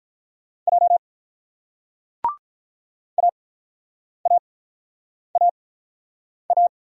Morse code, heard as a keyed tone, about 42 wpm, 700 Hz. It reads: J A A A A
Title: Speed-Racing Single Letters 35wpm